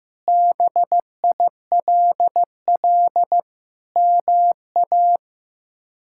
Morse code, heard as a keyed tone, about 15 wpm, 700 Hz. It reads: BILL MA
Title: K1USN SST QSO Practice - 15wpm